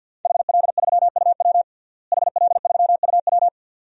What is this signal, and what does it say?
Morse code, heard as a keyed tone, about 49 wpm, 700 Hz. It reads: HL3UW HL3UW
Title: DXCC-100 Call Signs - 5 Characters - Repeated 2x - 50wpm